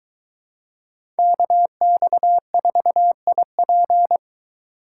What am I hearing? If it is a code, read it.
Morse code, heard as a keyed tone, about 22 wpm, 700 Hz. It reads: KX4IP